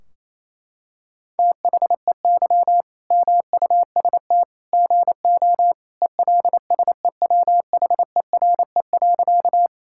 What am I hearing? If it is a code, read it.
Morse code, heard as a keyed tone, about 27 wpm, 700 Hz. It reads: THEY MUST GO ELSEWHERE.
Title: Sentences from Top 1600 Words 28wpm